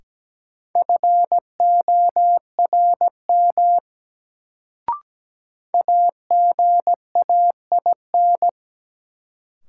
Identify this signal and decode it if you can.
Morse code, heard as a keyed tone, about 17 wpm, 700 Hz. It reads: FORM AGAIN
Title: Sets of 4 Words - New Word Drill - 17wpm